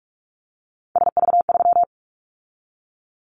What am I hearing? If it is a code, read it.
Morse code, heard as a keyed tone, about 46 wpm, 690 Hz. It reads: SV3
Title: Callsign Prefixes for Top 50 DXCC - 45wpm